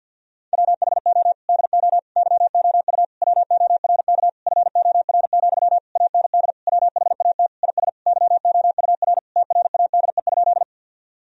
Morse code, heard as a keyed tone, about 50 wpm, 700 Hz. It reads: WHO DO YOU WORK FOR, AND WHAT IS YOUR TRADE?
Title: Sentences from Top 1000 Words - Review Part 4 - 50wpm